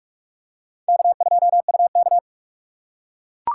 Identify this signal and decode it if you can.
Morse code, heard as a keyed tone, about 39 wpm, 700 Hz. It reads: K2UK